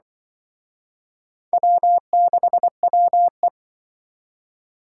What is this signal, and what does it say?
Morse code, heard as a keyed tone, about 25 wpm, 700 Hz. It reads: W6WE